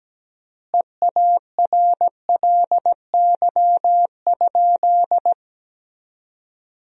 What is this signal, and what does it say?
Morse code, heard as a keyed tone, about 17 wpm, 700 Hz. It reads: EARLY?
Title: 3x Word-Spacing - Sentences from Top 400 Words - 17wpm